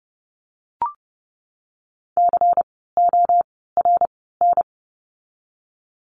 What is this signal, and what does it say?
Morse code, heard as a keyed tone, about 32 wpm, 700 Hz. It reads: C O R N
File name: Lesson 16f - Letter c - Words 30wpm